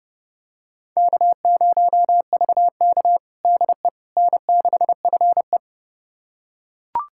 Morse code, heard as a keyed tone, about 30 wpm, 700 Hz. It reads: K0VK DE N6FE
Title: USA Callsign DE Callsign - 4 Characters - 30wpm